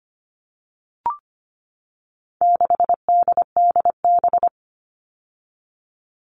Morse code, heard as a keyed tone, about 25 wpm, 700 Hz. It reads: 6DDB
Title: All Cut Numbers - 4-Digit 25wpm